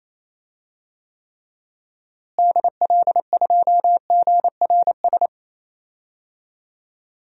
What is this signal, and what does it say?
Morse code, heard as a keyed tone, about 27 wpm, 700 Hz. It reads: DL2GRS
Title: DXCC-100 Call Signs - 6 Characters - Repeated 3x - 28wpm